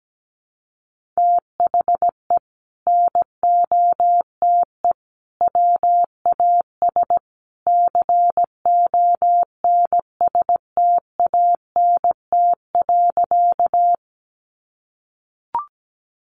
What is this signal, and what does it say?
Morse code, heard as a keyed tone, about 17 wpm, 700 Hz. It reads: THE NOTE WAS CONSTANT.
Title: Sentences from Top 1500 Words 17wpm